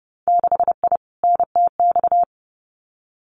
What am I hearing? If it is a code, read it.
Morse code, heard as a keyed tone, about 30 wpm, 700 Hz. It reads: 6I NTX
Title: Winter Field Day Exchanges Encore 30wpm